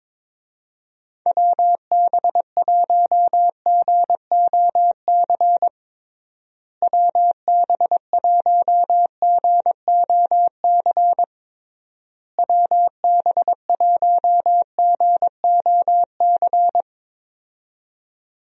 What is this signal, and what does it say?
Morse code, heard as a keyed tone, about 22 wpm, 700 Hz. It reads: WB1GOC WB1GOC WB1GOC
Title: USA Call Signs - 6 Characters - Repeated 3x - 22wpm